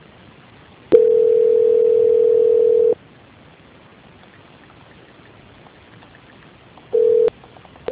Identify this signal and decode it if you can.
Ringback tone (North America), 440 + 480 Hz.